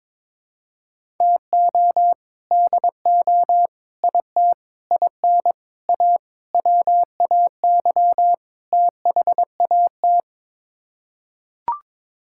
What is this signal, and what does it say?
Morse code, heard as a keyed tone, about 22 wpm, 700 Hz. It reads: TO DO IT IN A WAY THAT
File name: practice / 22wpm / Sets of 7 Words - 22wpm